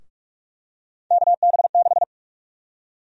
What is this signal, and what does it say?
Morse code, heard as a keyed tone, about 47 wpm, 700 Hz. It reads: KB6
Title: USA Call Sign Prefixes - 45wpm